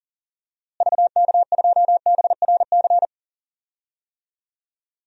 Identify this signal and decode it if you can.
Morse code, heard as a keyed tone, about 37 wpm, 700 Hz. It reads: VK2BRC
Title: DXCC-100 Call Signs - 6 Characters - Repeated 3x - 40wpm